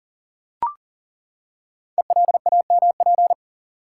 Morse code, heard as a keyed tone, about 40 wpm, 700 Hz. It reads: ELAMP
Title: Code Groups - 5 Letters - 40wpm